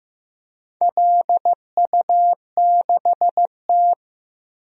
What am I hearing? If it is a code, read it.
Morse code, heard as a keyed tone, about 15 wpm, 700 Hz. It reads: LU6T
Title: DXCC-100 Call Signs - 4 Characters - Repeated 2x - 15wpm